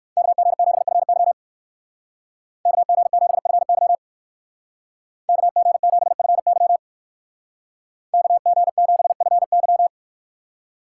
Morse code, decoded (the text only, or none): KG7FY KG7FY KG7FY KG7FY